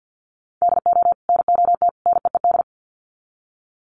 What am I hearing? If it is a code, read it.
Morse code, heard as a keyed tone, about 50 wpm, 700 Hz. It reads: DO NOT NEED
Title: Sets of 3 Words 50wpm